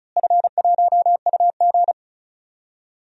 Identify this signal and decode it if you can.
Morse code, heard as a keyed tone, about 36 wpm, 700 Hz. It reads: F1UG
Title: DXCC-100 Call Signs - 4 Characters - Repeated 3x - 35wpm